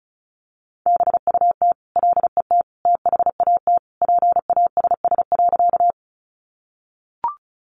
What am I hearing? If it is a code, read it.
Morse code, heard as a keyed tone, about 34 wpm, 700 Hz. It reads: BUT LET THAT PASS.
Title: Sentences from Top 400 Words 35wpm